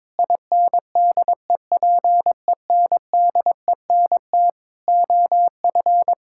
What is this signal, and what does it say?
Morse code, heard as a keyed tone, about 22 wpm, 700 Hz. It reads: INDEPENDENT OF
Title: Sentences from Top 1000 Words - Review Part 6 - 22wpm